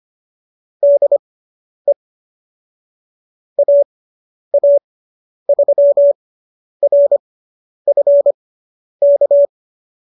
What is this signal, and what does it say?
Morse code, heard as a keyed tone, about 25 wpm, spaced as for 10 wpm, 570 Hz. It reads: DE AA3RF K